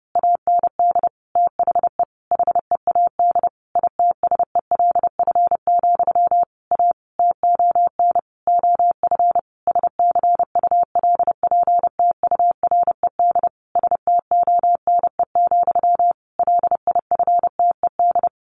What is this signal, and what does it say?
Morse code, heard as a keyed tone, about 30 wpm, 700 Hz. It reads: AND THE HEAD ITSELF, A TON OF SCULPTURED STONE, LIFTED
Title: A Sound Of Thunder 30wpm